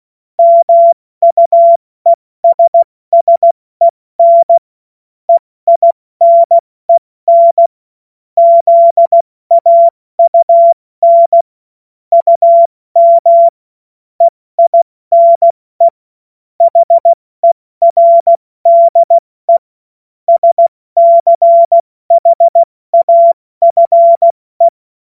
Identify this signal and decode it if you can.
Morse code, heard as a keyed tone, about 16 wpm, 680 Hz. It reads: MUESSEN EINEN ZAUN UM EINE HERDE SCHAFE
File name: Morsekurs / Witze / Witz 26 Dezember 2023 70 F80